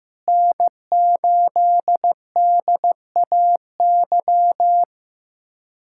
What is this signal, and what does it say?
Morse code, heard as a keyed tone, about 15 wpm, 700 Hz. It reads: N8DAY
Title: USA Callsign DE Callsign - 5 Characters - 15wpm